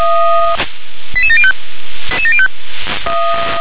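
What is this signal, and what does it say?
Five-tone selective call (ZVEI), heard as quick runs of five tones, tones of about 70 ms. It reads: 88705
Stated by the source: ZVEI five-tone selective call received on an SDR.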